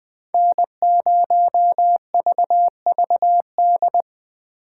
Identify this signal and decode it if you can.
Morse code, heard as a keyed tone, about 21 wpm, 700 Hz. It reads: N0VVD